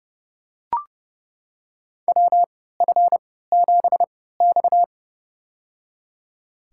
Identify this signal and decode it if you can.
Morse code, heard as a keyed tone, about 30 wpm, spaced as for 17 wpm, 700 Hz. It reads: WF7X
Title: USA Call Signs - 4 Characters - 3x Character Spacing - 30wpm